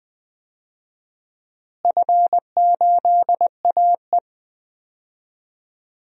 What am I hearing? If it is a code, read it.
Morse code, heard as a keyed tone, about 20 wpm, 700 Hz. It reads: F8AE
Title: DXCC-100 Call Signs - 4 Characters - Repeated 2x - 20wpm